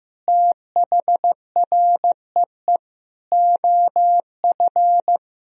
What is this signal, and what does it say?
Morse code, heard as a keyed tone, about 15 wpm, 700 Hz. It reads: THREE OF
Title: Sentences from Top 1000 Words - Review Part 3 - 15wpm